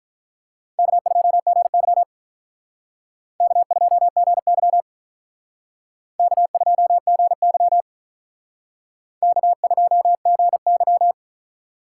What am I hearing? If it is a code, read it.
Morse code, heard as a keyed tone, about 46 wpm, 700 Hz. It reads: K2GY K2GY K2GY K2GY